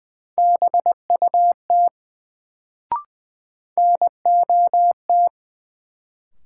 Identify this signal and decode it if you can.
Morse code, heard as a keyed tone, about 20 wpm, 700 Hz. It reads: BUT NOT
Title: Sets of 2 Words - New Word Drill - 20wpm